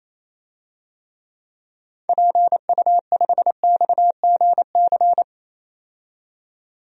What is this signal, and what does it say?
Morse code, heard as a keyed tone, about 28 wpm, 700 Hz. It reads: PU5XGC
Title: DXCC-100 Call Signs - 6 Characters - 28wpm